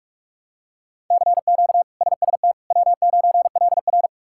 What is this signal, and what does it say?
Morse code, heard as a keyed tone, about 43 wpm, 700 Hz. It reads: CQ SST W9LR
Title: K1USN SST QSO Practice - 45wpm